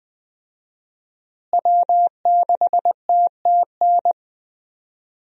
Morse code, heard as a keyed tone, about 20 wpm, 700 Hz. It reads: W6TTN
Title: USA Call Signs - 5 Characters - Repeated 2x - 20wpm